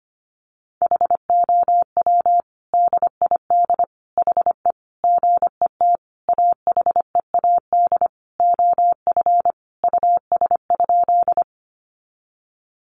Morse code, heard as a keyed tone, about 25 wpm, 700 Hz. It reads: HOW DID HE GET AHEAD OF US?
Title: Sentences from Top 1100 Words 25wpm